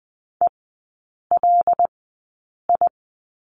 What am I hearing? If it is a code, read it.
Morse code, heard as a keyed tone, about 20 wpm, spaced as for 8 wpm, 700 Hz. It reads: ELI
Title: 100 Lesson 17e - Letter u - Three Characters 20wpm